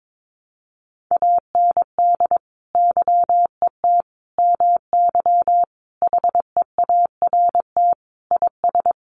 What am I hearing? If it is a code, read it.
Morse code, heard as a keyed tone, about 22 wpm, 700 Hz. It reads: AND YET MY HEART IS
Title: Sentences from Top 1200 Words 22wpm